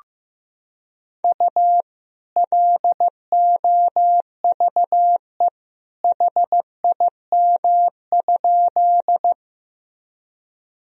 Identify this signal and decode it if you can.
Morse code, heard as a keyed tone, about 15 wpm, 700 Hz. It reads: U LOVE HIM?
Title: Sentences from Top 400 Words and Abbreviations 15wpm